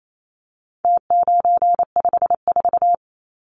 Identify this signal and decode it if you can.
Morse code, heard as a keyed tone, about 28 wpm, 700 Hz. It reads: T954
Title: Numbers - Contest Style - 4-Digit 28wpm